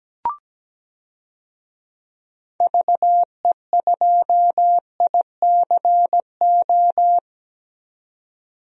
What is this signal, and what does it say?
Morse code, heard as a keyed tone, about 17 wpm, 700 Hz. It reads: VE2ICO
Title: DXCC-100 Call Signs - 6 Characters - 17wpm